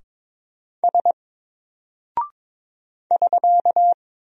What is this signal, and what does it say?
Morse code, heard as a keyed tone, about 22 wpm, 700 Hz. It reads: S <SK>
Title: Single Character (Letter:Number:Punctuation:Prosign) 22wpm